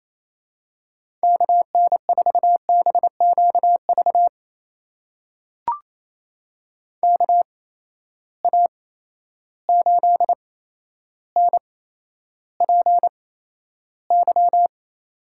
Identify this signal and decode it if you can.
Morse code, heard as a keyed tone, about 28 wpm, spaced as for 12 wpm, 700 Hz. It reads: KN4BQV KA8NPY
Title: USA Call Signs - 6 Characters - 8x Character Spacing - 28wpm